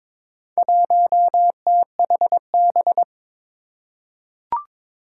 Morse code, heard as a keyed tone, about 22 wpm, 700 Hz. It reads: 1THB
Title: Four Challenging Letter-Number Combinations 22wpm